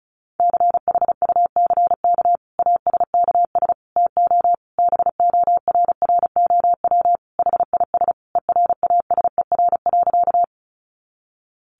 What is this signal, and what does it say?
Morse code, heard as a keyed tone, about 34 wpm, 700 Hz. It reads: CHUCK ASKS TO BORROW HIS ERASER.
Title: Additional Easy Phrases 35wpm